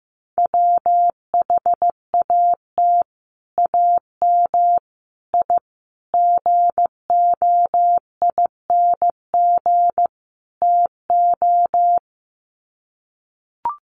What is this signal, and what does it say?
Morse code, heard as a keyed tone, about 15 wpm, 700 Hz. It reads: WHAT AM I GOING TO
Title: Sets of 5 Words Encore 15wpm